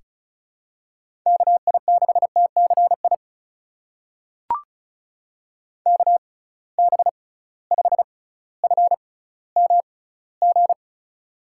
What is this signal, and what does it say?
Morse code, heard as a keyed tone, about 36 wpm, spaced as for 18 wpm, 700 Hz. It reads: KI6TCI KB5FMG
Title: USA Call Signs - 6x Character Spacing - 35wpm